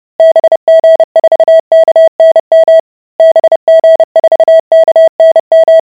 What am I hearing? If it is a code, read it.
Morse code, heard as a keyed tone, about 30 wpm, 650 Hz. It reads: BG4KNM BG4KNM